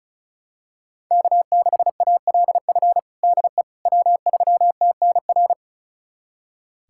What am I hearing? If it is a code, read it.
Morse code, heard as a keyed tone, about 34 wpm, 700 Hz. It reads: K6ALF DE W3TNR
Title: USA Callsign DE Callsign - 5 Characters - 35wpm